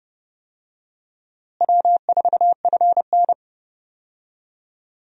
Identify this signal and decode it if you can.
Morse code, heard as a keyed tone, about 32 wpm, 700 Hz. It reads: W4FN
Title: USA Call Signs - 4 Characters - Repeated 2x - 30wpm